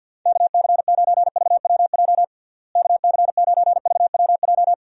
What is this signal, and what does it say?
Morse code, heard as a keyed tone, about 52 wpm, 700 Hz. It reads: KC9VWJ KC9VWJ
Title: USA Call Signs - 6 Characters - Repeated 3x - 50wpm